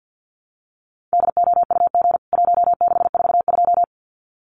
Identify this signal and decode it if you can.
Morse code, heard as a keyed tone, about 50 wpm, 700 Hz. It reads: DOUG 1642
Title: CWT Exchange 50wpm